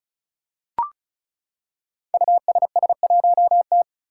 Morse code, heard as a keyed tone, about 36 wpm, 700 Hz. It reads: USS1T
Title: Five Challenging Letter-Number Combinations 35wpm